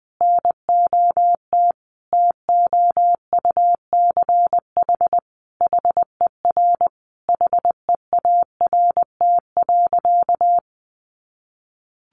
Morse code, heard as a keyed tone, about 20 wpm, 700 Hz. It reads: NOT TOUCH HER HEART.
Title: Sentences from Top 1200 Words 20wpm